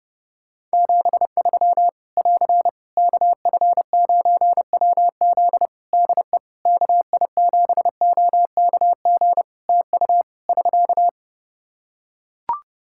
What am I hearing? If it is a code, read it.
Morse code, heard as a keyed tone, about 28 wpm, 700 Hz. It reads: 73 <AR> KF9WZ DE KI7OKG TU <SK>
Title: K1USN SST QSO Practice 30wpm